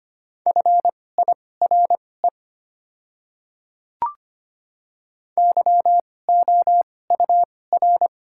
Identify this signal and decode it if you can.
Morse code, heard as a keyed tone, about 25 wpm, spaced as for 18 wpm, 700 Hz. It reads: FIRE YOUR
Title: 2x Character Spacing - Top 500 Words - 25wpm